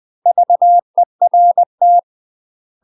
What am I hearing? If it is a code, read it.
Morse code, heard as a keyed tone, about 20 wpm, 700 Hz. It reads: VERT
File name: UK MASTER QSOs TRAINING / 20WPM / UK MASTER QSOs 0066